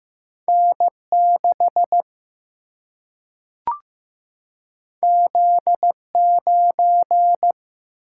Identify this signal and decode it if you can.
Morse code, heard as a keyed tone, about 15 wpm, 700 Hz. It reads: N 6 Z 9
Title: Single-Letter Single-Number 15wpm